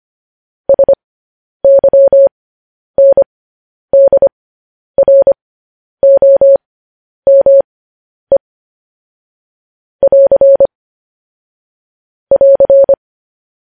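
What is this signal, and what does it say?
Morse code, heard as a keyed tone, about 25 wpm, spaced as for 10 wpm, 550 Hz. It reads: SYNDROME <AR> <AR>
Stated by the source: Morse code news headlines practice, at 10 WPM.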